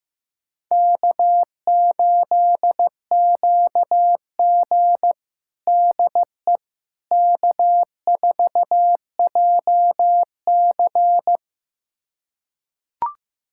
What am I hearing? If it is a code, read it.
Morse code, heard as a keyed tone, about 15 wpm, 700 Hz. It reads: K8QG DE K4JC
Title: USA Callsign DE Callsign - 4 Characters - 15wpm